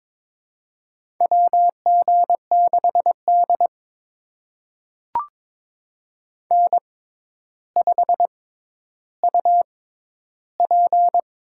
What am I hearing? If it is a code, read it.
Morse code, heard as a keyed tone, about 22 wpm, spaced as for 11 wpm, 700 Hz. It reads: WG6D N5UP
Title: USA Call Signs - 4 Characters - 6x Character Spacing - 22wpm